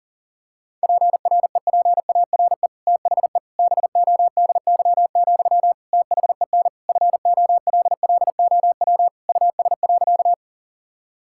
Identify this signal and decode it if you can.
Morse code, heard as a keyed tone, about 40 wpm, 700 Hz. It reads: PREPARE THE BODY, THEN FOLLOW US.
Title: Sentences from Top 1000 Words - Review Part 6 - 40wpm